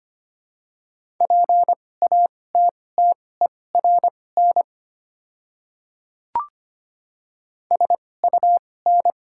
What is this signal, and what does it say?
Morse code, heard as a keyed tone, about 25 wpm, spaced as for 18 wpm, 700 Hz. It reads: PATTERN SUN
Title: 2x Character Spacing - Top 500 Words - 25wpm